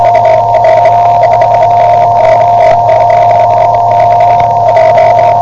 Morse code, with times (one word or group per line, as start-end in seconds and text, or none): none